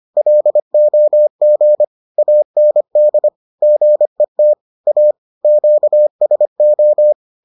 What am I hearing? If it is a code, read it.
Morse code, heard as a keyed tone, about 25 wpm, 600 Hz. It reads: LOG AND GET A QSO